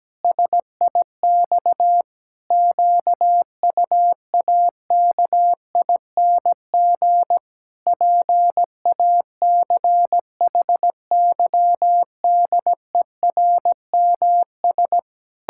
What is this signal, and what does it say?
Morse code, heard as a keyed tone, about 17 wpm, 700 Hz. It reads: SIX QUAKING PACHYDERMS
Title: Pangram Challenge v2 - 17wpm